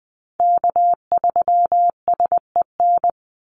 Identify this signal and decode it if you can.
Morse code, heard as a keyed tone, about 20 wpm, 700 Hz. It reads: K3SEN
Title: Call Signs Repeated 3x - 20wpm